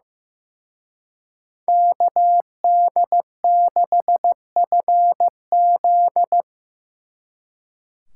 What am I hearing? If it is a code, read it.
Morse code, heard as a keyed tone, about 18 wpm, 700 Hz. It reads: KD6FZ
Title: Speed Racing - USA 2x2 Call Signs 15wpm